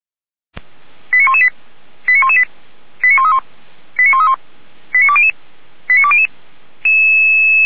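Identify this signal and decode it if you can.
Five-tone selective call (ZVEI), heard as quick runs of five tones, tones of about 70 ms. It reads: 89118 89118 89131 89131 89200 89200
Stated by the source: ZVEI five-tone selective call received on SDR.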